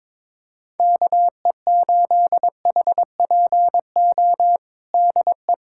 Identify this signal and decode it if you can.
Morse code, heard as a keyed tone, about 21 wpm, 700 Hz. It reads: KE8HPO DE
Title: USA Callsign DE Callsign - 6 Characters - 22wpm